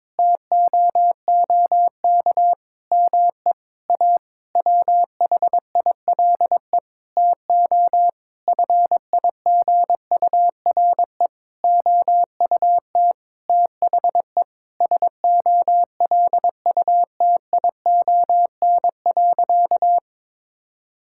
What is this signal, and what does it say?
Morse code, heard as a keyed tone, about 22 wpm, 700 Hz. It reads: TOOK ME A WHILE TO FIGURE OUT THE SOLUTION.
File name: practice / 22wpm / Sentences with 8-Word Phrases - 22wpm